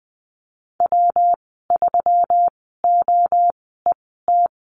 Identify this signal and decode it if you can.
Morse code, heard as a keyed tone, about 20 wpm, spaced as for 15 wpm, 700 Hz. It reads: W3OET
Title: 122 Lesson 20g - Number 3 - Call Signs 20wpm